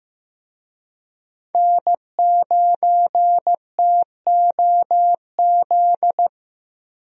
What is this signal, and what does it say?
Morse code, heard as a keyed tone, about 15 wpm, 700 Hz. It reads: N9TOZ
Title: USA Call Signs - 5 Characters - Repeated 3x - 15wpm